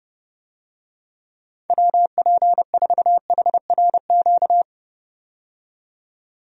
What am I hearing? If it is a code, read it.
Morse code, heard as a keyed tone, about 30 wpm, 700 Hz. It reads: WP4HRQ
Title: USA Call Signs - 6 Characters - 30wpm